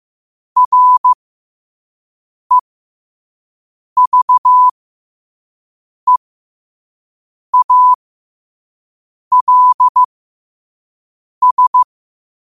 Morse code, heard as a keyed tone, about 15 wpm, spaced as for 5 wpm, 1000 Hz. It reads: REVEALS
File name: qotdfiles / 5wpm